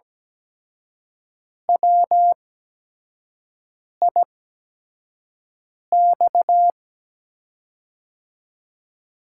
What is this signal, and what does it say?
Morse code, heard as a keyed tone, about 17 wpm, spaced as for 5 wpm, 700 Hz. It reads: WIX
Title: USA Call Signs - 8x Character Spacing - 17wpm